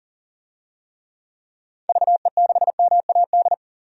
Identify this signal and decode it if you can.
Morse code, heard as a keyed tone, about 40 wpm, 700 Hz. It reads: VE6MAD
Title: DXCC-100 Call Signs - 6 Characters - 40wpm